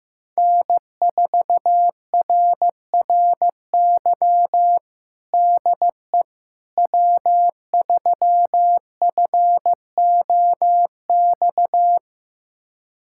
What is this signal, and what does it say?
Morse code, heard as a keyed tone, about 15 wpm, 700 Hz. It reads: N4RRY DE W3FOX
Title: USA Callsign DE Callsign - 5 Characters - 15wpm